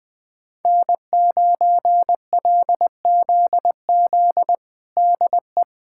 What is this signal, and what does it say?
Morse code, heard as a keyed tone, about 20 wpm, 700 Hz. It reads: N9LZZ DE
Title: USA Callsign DE Callsign - 5 Characters - 20wpm